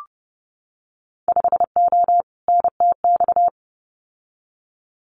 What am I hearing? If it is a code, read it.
Morse code, heard as a keyed tone, about 30 wpm, 700 Hz. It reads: 5O NTX
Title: Winter Field Day Exchanges Encore 30wpm